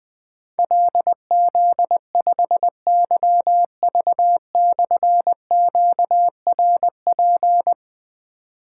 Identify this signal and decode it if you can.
Morse code, heard as a keyed tone, about 20 wpm, 700 Hz. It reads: LZ5YV/QRP
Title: Difficult Call Signs 20wpm